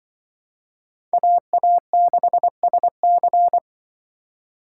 Morse code, heard as a keyed tone, about 25 wpm, 700 Hz. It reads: AA6SC